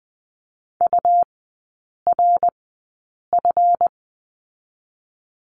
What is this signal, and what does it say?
Morse code, heard as a keyed tone, about 20 wpm, spaced as for 8 wpm, 700 Hz. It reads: URF
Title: 138 Lesson 23e - Letter f - Three Characters 20wpm